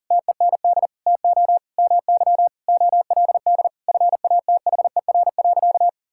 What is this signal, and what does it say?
Morse code, heard as a keyed tone, about 37 wpm, 700 Hz. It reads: TEND TO MY OLD FATHER.
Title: Sentences from Top 1000 Words - Review Part 5 - 40wpm